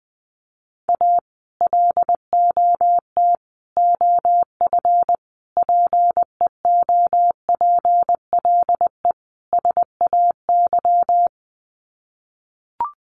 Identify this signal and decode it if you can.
Morse code, heard as a keyed tone, about 20 wpm, 700 Hz. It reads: A LOT OF PEOPLE SAY
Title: Sets of 5 Words Encore2 20wpm